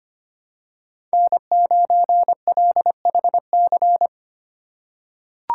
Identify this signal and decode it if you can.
Morse code, heard as a keyed tone, about 25 wpm, 700 Hz. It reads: N9LHC
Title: USA Call Signs - 5 Characters - 2x Character Spacing - 25wpm